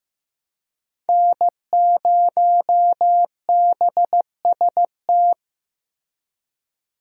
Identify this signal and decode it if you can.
Morse code, heard as a keyed tone, about 15 wpm, 700 Hz. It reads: N0BST
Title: USA Call Signs - 5 Characters - Repeated 2x - 15wpm